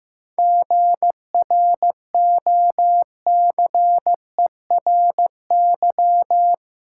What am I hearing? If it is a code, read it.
Morse code, heard as a keyed tone, about 15 wpm, 700 Hz. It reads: GROCERY
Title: Seven Letter Words - 15wpm